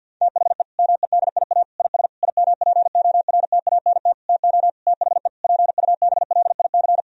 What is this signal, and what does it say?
Morse code, heard as a keyed tone, about 50 wpm, 700 Hz. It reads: THE MEDIA IS IMPORTANT TO THE PUBLIC
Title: Sentences from Top 1000 Words - Review Part 4 - 50wpm